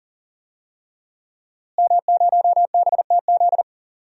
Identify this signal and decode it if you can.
Morse code, heard as a keyed tone, about 40 wpm, 700 Hz. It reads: M0BTZ
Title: DXCC-100 Call Signs - 5 Characters - Repeated 3x - 40wpm